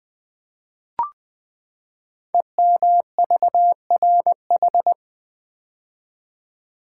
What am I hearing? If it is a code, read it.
Morse code, heard as a keyed tone, about 20 wpm, 700 Hz. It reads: EMVRH
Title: Code Groups - 5 Letters - 20wpm